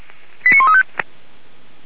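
Five-tone selective call (ZVEI), heard as quick runs of five tones, tones of about 70 ms. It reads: 89135
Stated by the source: ZVEI five-tone selective call received by SDR.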